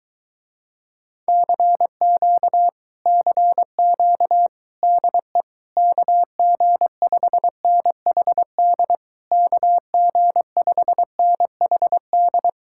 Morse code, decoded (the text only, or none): CQ CQ DE KG5NHD KG5NHD